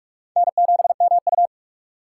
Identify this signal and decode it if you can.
Morse code, heard as a keyed tone, about 46 wpm, 700 Hz. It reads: N7MU
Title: USA Call Signs - 2x Character Spacing - 45wpm